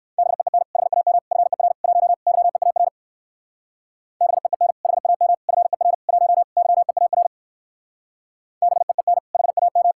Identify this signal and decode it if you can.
Morse code, heard as a keyed tone, about 56 wpm, 700 Hz. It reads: BEEN HAM FER 1 YEAR BEEN HAM FER 1 YEAR BEEN HAM